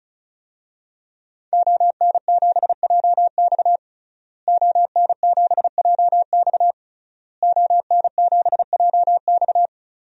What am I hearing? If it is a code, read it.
Morse code, heard as a keyed tone, about 34 wpm, 700 Hz. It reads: ON7JX ON7JX ON7JX
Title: DXCC-100 Call Signs - 5 Characters - Repeated 3x - 35wpm